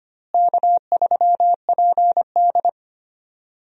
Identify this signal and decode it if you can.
Morse code, heard as a keyed tone, about 25 wpm, 700 Hz. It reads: K3PD
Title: USA Call Signs - 4 Characters - Repeated 3x - 25wpm